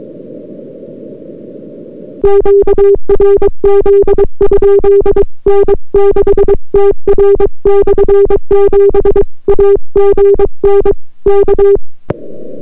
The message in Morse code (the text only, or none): QRZ? N6TR/7 AGN K